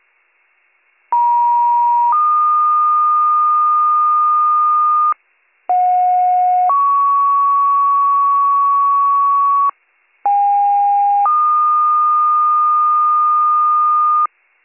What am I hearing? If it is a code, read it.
Two-tone page, heard as a one-second tone, then a three-second tone, about 950 then 1220 Hz, about 730 then 1120 Hz, about 810 then 1220 Hz. It is heard from 1.1 to 14.3 s.